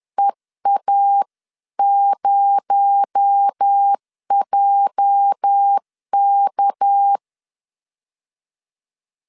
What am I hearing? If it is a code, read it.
Morse code, heard as a keyed tone, about 10 wpm, 800 Hz. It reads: EA0JK
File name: CW / DET / 1 9 1500C S A 10wpm 4511